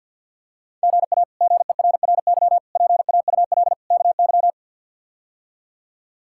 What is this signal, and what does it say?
Morse code, heard as a keyed tone, about 51 wpm, 700 Hz. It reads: GA GERRY PAUL KY
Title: K1USN SST QSO Practice - 50wpm